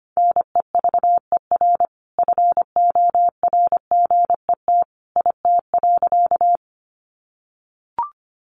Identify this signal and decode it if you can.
Morse code, heard as a keyed tone, about 25 wpm, 700 Hz. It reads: NEVER FORGET IT.
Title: Sentences from Top 1200 Words 25wpm